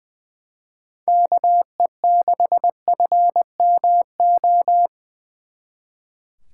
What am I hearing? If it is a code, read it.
Morse code, heard as a keyed tone, about 24 wpm, 700 Hz. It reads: KE6FMO
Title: Speed Racing - USA Call Signs 20wpm